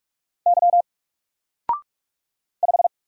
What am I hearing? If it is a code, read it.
Morse code, heard as a keyed tone, about 46 wpm, 700 Hz. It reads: Y 5
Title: Single Character (Letter:Number:Punctuation:Prosign) 45wpm